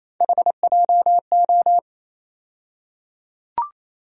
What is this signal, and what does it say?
Morse code, heard as a keyed tone, about 28 wpm, 700 Hz. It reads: HJO
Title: USA Call Sign Postfixes - 3 Letters 28wpm